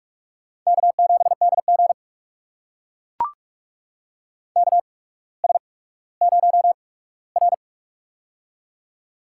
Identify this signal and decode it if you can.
Morse code, heard as a keyed tone, about 41 wpm, spaced as for 19 wpm, 700 Hz. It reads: K7DG KS0R
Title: USA Call Signs - 4 Characters - 8x Character Spacing - 45wpm